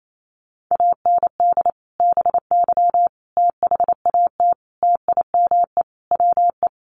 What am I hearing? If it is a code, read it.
Morse code, heard as a keyed tone, about 28 wpm, 700 Hz. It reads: AND BY THAT TIME WE
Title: Sentences from Top 1100 Words 28wpm